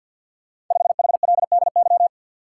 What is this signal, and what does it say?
Morse code, heard as a keyed tone, about 52 wpm, 690 Hz. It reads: 5HLDY
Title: Code Groups - 5 Letter:Numbers - 50wpm